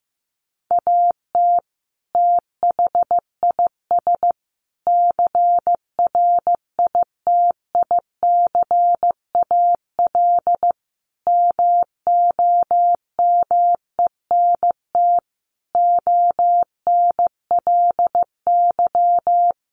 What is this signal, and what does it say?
Morse code, heard as a keyed tone, about 15 wpm, 700 Hz. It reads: AT THIS CRITICAL MOMENT ONLY
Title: Sentences from Top 1100 Words 15wpm